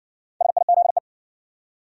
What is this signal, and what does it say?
Morse code, heard as a keyed tone, about 60 wpm, 700 Hz. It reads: SIZE